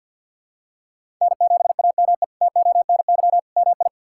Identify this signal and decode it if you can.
Morse code, heard as a keyed tone, about 46 wpm, 700 Hz. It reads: N7AME TONY MI
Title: K1USN SST QSO Practice - 50wpm